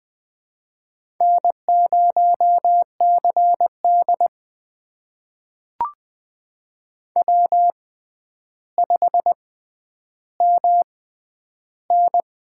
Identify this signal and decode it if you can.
Morse code, heard as a keyed tone, about 21 wpm, spaced as for 10 wpm, 700 Hz. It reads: N0CD W5MN
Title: USA Call Signs - 6x Character Spacing - 20wpm